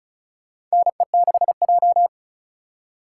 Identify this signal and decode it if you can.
Morse code, heard as a keyed tone, about 36 wpm, 700 Hz. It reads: NE6J
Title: USA Call Signs - 4x Character Spacing - 35wpm